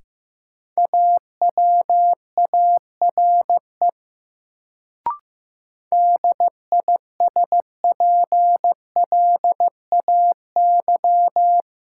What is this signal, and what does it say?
Morse code, heard as a keyed tone, about 15 wpm, 700 Hz. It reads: AWARE DISPLAY
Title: Sentences from Top 1200 Words - New Word Drill Alt - 15wpm